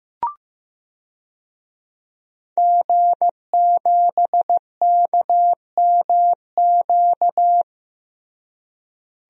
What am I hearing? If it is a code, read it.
Morse code, heard as a keyed tone, about 15 wpm, 700 Hz. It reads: G7KMQ
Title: DXCC-100 Call Signs - 5 Characters - Repeated 2x - 15wpm